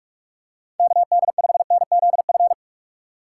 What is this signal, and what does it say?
Morse code, heard as a keyed tone, about 44 wpm, 700 Hz. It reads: KD5NZF
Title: USA Call Signs - 6 Characters - 6x Character Spacing - 45wpm